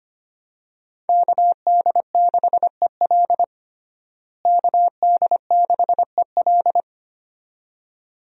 Morse code, heard as a keyed tone, about 25 wpm, 700 Hz. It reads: KD6EL KD6EL
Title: USA Call Signs - 5 Characters - Repeated 2x - 25wpm